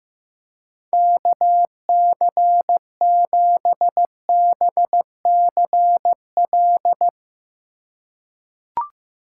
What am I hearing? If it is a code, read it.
Morse code, heard as a keyed tone, about 15 wpm, 700 Hz. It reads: KC7BCL
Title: USA Call Signs - 3x Character Spacing - 15wpm